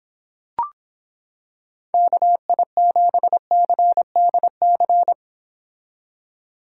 Morse code, heard as a keyed tone, about 26 wpm, 700 Hz. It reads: KI7CDC